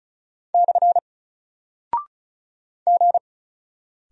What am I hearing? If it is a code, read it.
Morse code, heard as a keyed tone, about 36 wpm, 700 Hz. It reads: / G
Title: Single Character (Letter:Number:Punctuation:Prosign) 35wpm